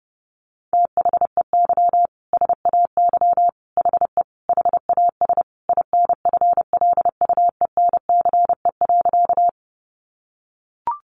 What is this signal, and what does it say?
Morse code, heard as a keyed tone, about 30 wpm, 700 Hz. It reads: THEY SAY HE HAS INFLUENCE.
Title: Sentences from Top 1100 Words 30wpm